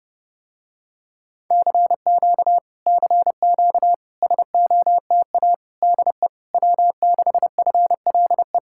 Morse code, heard as a keyed tone, about 32 wpm, 700 Hz. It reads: CQ CQ SOTA DE W6FLE